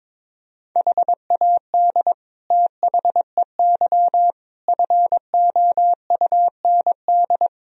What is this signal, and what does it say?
Morse code, heard as a keyed tone, about 22 wpm, 700 Hz. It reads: HAD THEY FOUND
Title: Sentences from Top 1000 Words - Review Part 2 - 22wpm